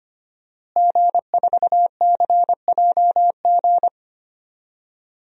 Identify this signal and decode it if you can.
Morse code, heard as a keyed tone, about 25 wpm, 700 Hz. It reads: G4CJG
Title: DXCC-100 Call Signs - 5 Characters - Repeated 2x - 25wpm